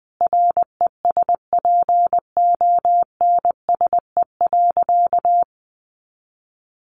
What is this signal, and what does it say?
Morse code, heard as a keyed tone, about 20 wpm, 700 Hz. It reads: RESPONSE.
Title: Sentences from Top 1400 Words 20wpm